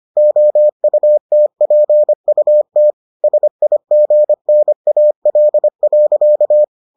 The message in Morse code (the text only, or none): OUTPUT SIGNAL.